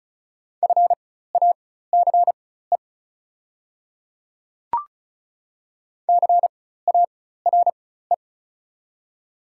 Morse code, heard as a keyed tone, about 36 wpm, spaced as for 16 wpm, 700 Hz. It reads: FACE CARE
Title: 4x Character Spacing - Top 300 Words - 35wpm